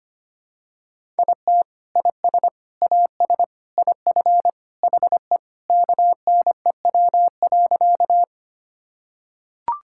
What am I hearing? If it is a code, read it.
Morse code, heard as a keyed tone, about 25 wpm, 700 Hz. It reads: IT IS AS IF HE KNEW.
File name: practice / 25wpm / Sentences with 3-Word Phrases - 25wpm